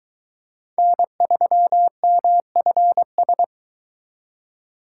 Morse code, heard as a keyed tone, about 23 wpm, 700 Hz. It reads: N3MFS